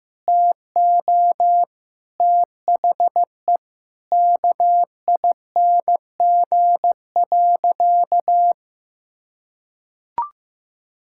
Morse code, heard as a keyed tone, about 15 wpm, 700 Hz. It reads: TO THE KING.
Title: Sentences from Top 1000 Words - Review Part 6 - 15wpm